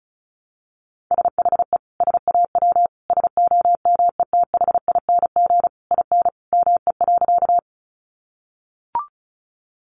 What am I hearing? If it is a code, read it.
Morse code, heard as a keyed tone, about 34 wpm, 700 Hz. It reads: SHE SAW SOMETHING IN ME.
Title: Sentences from Top 600 Words 35wpm